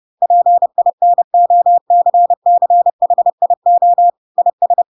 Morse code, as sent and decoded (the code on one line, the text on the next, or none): .--. .. -. --- -.-. -.-. .... .. --- / .. ...
PINOCCHIO IS